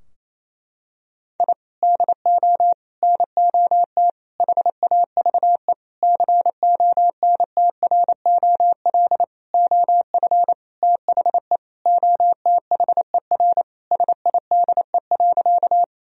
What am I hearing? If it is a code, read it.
Morse code, heard as a keyed tone, about 28 wpm, 700 Hz. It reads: I DO NOT HAVE CONTROL OF THE OTHER SIDE.
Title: Sentences from Top 1000 Words - Review Part 3 - 28wpm